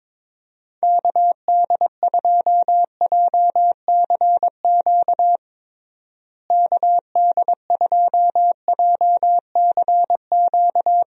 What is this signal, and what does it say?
Morse code, heard as a keyed tone, about 22 wpm, 700 Hz. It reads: KD2JCQ KD2JCQ
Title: USA Call Signs - 6 Characters - Repeated 3x - 22wpm